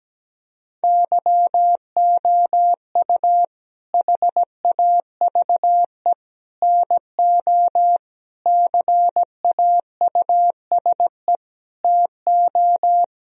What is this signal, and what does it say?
Morse code, heard as a keyed tone, about 17 wpm, 700 Hz. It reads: YOU HAVE NO CAUSE TO
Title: Sentences from Top 1000 Words - Review Part 1 - 17wpm